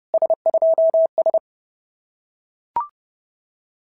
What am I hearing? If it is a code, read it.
Morse code, heard as a keyed tone, about 28 wpm, 650 Hz. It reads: S2S
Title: SOTA QSO Practice 30wpm